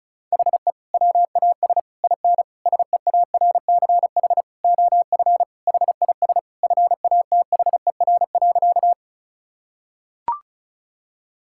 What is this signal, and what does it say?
Morse code, heard as a keyed tone, about 36 wpm, 700 Hz. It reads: HE WAS IN SEARCH OF HIS FATHER.
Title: Sentences from Top 1000 Words - Review Part 5 - 35wpm